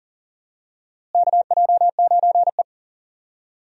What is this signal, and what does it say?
Morse code, heard as a keyed tone, about 40 wpm, 700 Hz. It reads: KJ9E
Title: USA Call Signs - 4 Characters - Repeated 3x - 40wpm